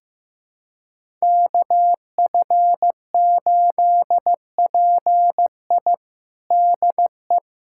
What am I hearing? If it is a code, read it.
Morse code, heard as a keyed tone, about 15 wpm, 700 Hz. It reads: KF8PI DE
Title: USA Callsign DE Callsign - 5 Characters - 15wpm